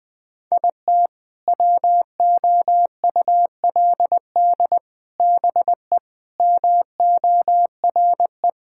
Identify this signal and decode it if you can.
Morse code, heard as a keyed tone, about 20 wpm, 700 Hz. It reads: IT WOULD BE MORE
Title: Sets of 8 Words - 20wpm